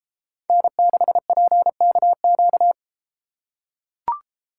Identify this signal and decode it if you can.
Morse code, heard as a keyed tone, about 30 wpm, 700 Hz. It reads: N6PKQ